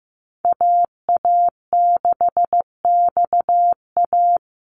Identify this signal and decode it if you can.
Morse code, heard as a keyed tone, about 15 wpm, 700 Hz. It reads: AA6XA
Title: Call Signs Repeated 3x - 15wpm